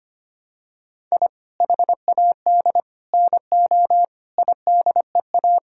Morse code, heard as a keyed tone, about 25 wpm, 700 Hz. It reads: I HAD NO IDEA
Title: Sentences from Top 1000 Words - Review Part 4 - 25wpm